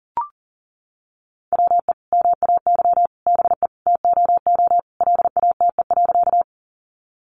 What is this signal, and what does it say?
Morse code, heard as a keyed tone, about 40 wpm, 700 Hz. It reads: WE MAY BE TOO LATE.
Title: Sentences from Top 300 Words 40wpm